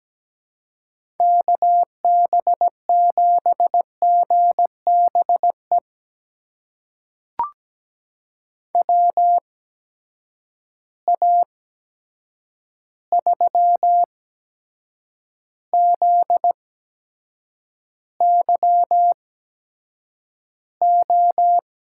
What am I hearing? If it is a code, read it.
Morse code, heard as a keyed tone, about 17 wpm, spaced as for 7 wpm, 700 Hz. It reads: KB7GBE WA3ZYO
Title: USA Call Signs - 6 Characters - 8x Character Spacing - 17wpm